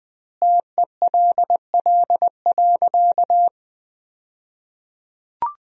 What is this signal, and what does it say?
Morse code, heard as a keyed tone, about 20 wpm, 700 Hz. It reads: TELL.
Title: 2x Word-Spacing - Sentences from Top 300 Words - 20wpm